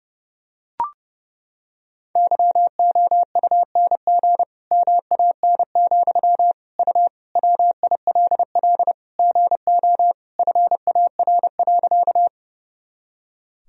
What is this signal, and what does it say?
Morse code, heard as a keyed tone, about 30 wpm, 700 Hz. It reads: YOUNG MAN, U WILL GO FAR.
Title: Sentences from Top 400 Words and Abbreviations 30wpm